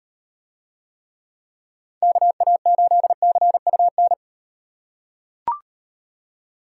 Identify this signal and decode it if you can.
Morse code, heard as a keyed tone, about 37 wpm, 700 Hz. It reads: KA8CUN